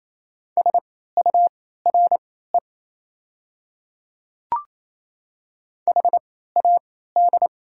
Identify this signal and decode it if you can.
Morse code, heard as a keyed tone, about 27 wpm, spaced as for 16 wpm, 700 Hz. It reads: SURE HAD
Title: 3x Character Spacing - Top 500 Words - 28wpm